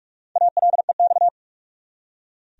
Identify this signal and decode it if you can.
Morse code, heard as a keyed tone, about 46 wpm, 700 Hz. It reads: ALEX
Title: USA Common Names - Max 5 Letters - 45wpm